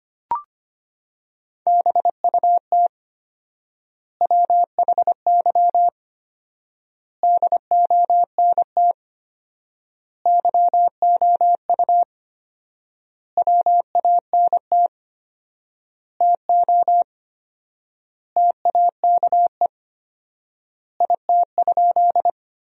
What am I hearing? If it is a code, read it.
Morse code, heard as a keyed tone, about 25 wpm, 700 Hz. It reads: BUT WHY DONT YOU WANT TO TAKE IT?
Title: 4x Word-Spacing - Sentences from Top 300 Words - 25wpm